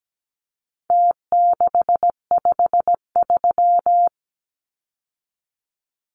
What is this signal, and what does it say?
Morse code, heard as a keyed tone, about 17 wpm, 700 Hz. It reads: T653
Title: Numbers - Contest Style - 4-Digit 17wpm